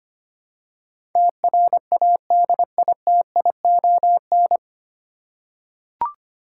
Morse code, heard as a keyed tone, about 25 wpm, 700 Hz. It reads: TRADITION
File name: practice / 25wpm / 9 Letter Words - First 3 Letters Given - 25wpm